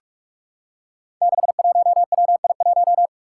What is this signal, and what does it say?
Morse code, heard as a keyed tone, about 46 wpm, 700 Hz. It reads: 61WI1
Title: Five Challenging Letter-Number Combinations 45wpm